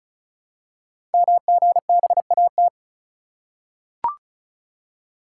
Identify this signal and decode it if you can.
Morse code, heard as a keyed tone, about 36 wpm, 700 Hz. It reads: MGBAT
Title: Code Groups - 5 Letter:Numbers - 35wpm